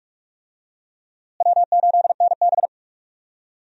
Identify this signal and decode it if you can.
Morse code, heard as a keyed tone, about 44 wpm, 700 Hz. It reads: W8NB
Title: USA Call Signs - 4 Characters - Repeated 2x - 45wpm